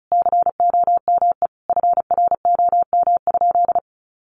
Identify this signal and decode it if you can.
Morse code, heard as a keyed tone, about 34 wpm, 700 Hz. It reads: COME FROM?
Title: Sentences from Top 300 Words 35wpm